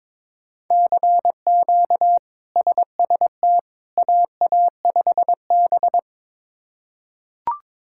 Morse code, heard as a keyed tone, about 21 wpm, 700 Hz. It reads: CQ SST AA5B
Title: K1USN SST QSO Practice - 22wpm